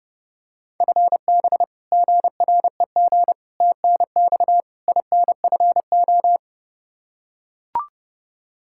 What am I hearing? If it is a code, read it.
Morse code, heard as a keyed tone, about 32 wpm, 700 Hz. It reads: FB GREG TNX INFO
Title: Practice QSO - Exchange 3 - More Info - 30wpm